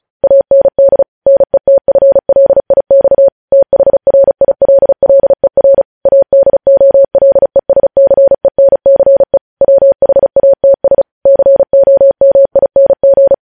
Morse code, heard as a keyed tone, about 36 wpm, 550 Hz. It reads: AND NETFLIX THRILLER ADOLESCENCE WHATS COMING
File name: podcast / News headlines-1x-35x35